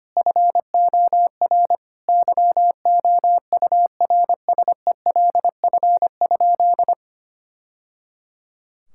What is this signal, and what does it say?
Morse code, heard as a keyed tone, about 25 wpm, 700 Hz. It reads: FOR YOURSELF?
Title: Sentences from Top 1000 Words - Review Part 6 - 25wpm